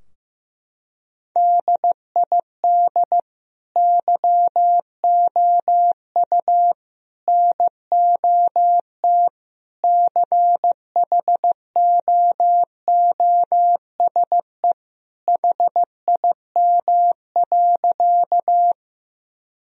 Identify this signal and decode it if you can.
Morse code, heard as a keyed tone, about 15 wpm, 700 Hz. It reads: DID YOU NOT CHOOSE HIM.
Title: Sentences from Top 1000 Words - Review Part 4 - 15wpm